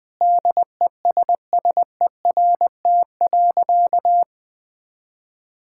Morse code, heard as a keyed tone, about 20 wpm, 700 Hz. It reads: DESSERT.
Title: Sentences with 5-Word Phrases - 20wpm